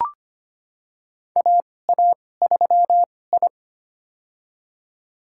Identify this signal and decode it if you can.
Morse code, heard as a keyed tone, about 25 wpm, spaced as for 18 wpm, 700 Hz. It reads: AA3I
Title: USA Call Signs - 4 Characters - 2x Character Spacing - 25wpm